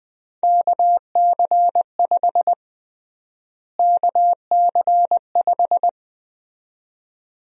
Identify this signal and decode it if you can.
Morse code, heard as a keyed tone, about 20 wpm, 700 Hz. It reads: KC5 KC5
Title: USA Call Sign Prefixes - Repeated 2x - 20wpm